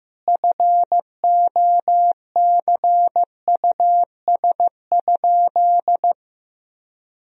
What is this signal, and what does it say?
Morse code, heard as a keyed tone, about 15 wpm, 700 Hz. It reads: FOCUS?
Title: Sentences from Top 1000 Words - Review Part 4 - 15wpm